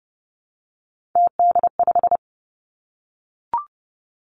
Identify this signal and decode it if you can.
Morse code, heard as a keyed tone, about 30 wpm, 700 Hz. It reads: TD5
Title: Three Challenging Letter-Number Combinations 30wpm